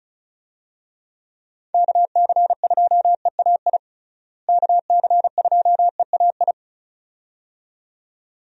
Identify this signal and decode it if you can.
Morse code, heard as a keyed tone, about 36 wpm, 700 Hz. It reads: KC2EAI KC2EAI
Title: USA Call Signs - 6 Characters - Repeated 2x - 35wpm